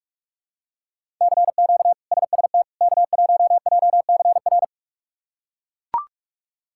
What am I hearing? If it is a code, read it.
Morse code, heard as a keyed tone, about 42 wpm, 700 Hz. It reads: CQ SST K1JCR
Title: K1USN SST QSO Practice - 45wpm